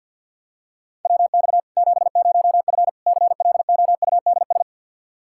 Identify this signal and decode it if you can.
Morse code, heard as a keyed tone, about 52 wpm, 700 Hz. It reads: WX 70F CLOUDS
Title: Practice QSO - Exchange 2 - Info - 50wpm